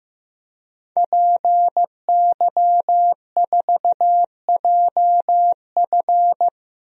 Morse code, heard as a keyed tone, about 15 wpm, 700 Hz. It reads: PY4JF
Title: DXCC-100 Call Signs - 5 Characters - 15wpm